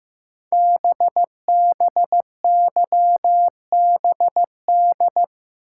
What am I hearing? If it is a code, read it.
Morse code, heard as a keyed tone, about 15 wpm, 700 Hz. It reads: BBYBD
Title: Code Groups - 5 Letters - 15wpm